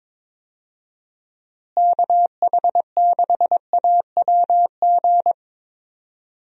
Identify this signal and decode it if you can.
Morse code, heard as a keyed tone, about 22 wpm, 700 Hz. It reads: KH6AWG
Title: DXCC-100 Call Signs - 6 Characters - Repeated 2x - 22wpm